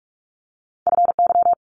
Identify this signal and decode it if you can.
Morse code, heard as a keyed tone, about 46 wpm, 700 Hz. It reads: FY
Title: Multi-letter Phonemes Quick Fire 45wpm